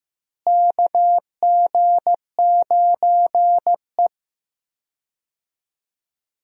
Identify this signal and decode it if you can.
Morse code, heard as a keyed tone, about 15 wpm, 700 Hz. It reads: KG9E
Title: USA Call Signs - 4 Characters - Repeated 2x - 15wpm